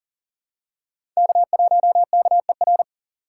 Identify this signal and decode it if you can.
Morse code, heard as a keyed tone, about 37 wpm, 700 Hz. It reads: K1KER
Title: USA Call Signs - 5 Characters - 4x Character Spacing - 40wpm